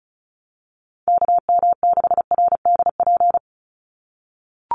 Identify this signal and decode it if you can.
Morse code, heard as a keyed tone, about 34 wpm, 700 Hz. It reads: KM6RDP
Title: USA Call Signs 35wpm